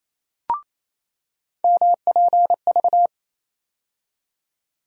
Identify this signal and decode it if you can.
Morse code, heard as a keyed tone, about 28 wpm, 700 Hz. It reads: MPV
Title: USA Call Sign Postfixes - 3 Letters 28wpm